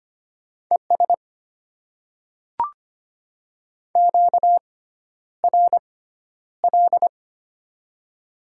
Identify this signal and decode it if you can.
Morse code, heard as a keyed tone, about 25 wpm, spaced as for 11 wpm, 700 Hz. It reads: ES QRL
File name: Top 20 QSO Elements - 6x Character Spacing - 25wpm